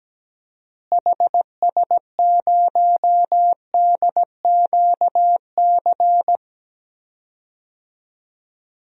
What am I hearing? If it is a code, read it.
Morse code, heard as a keyed tone, about 17 wpm, 700 Hz. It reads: HS0DQC
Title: DXCC-100 Call Signs - 6 Characters - Repeated 2x - 17wpm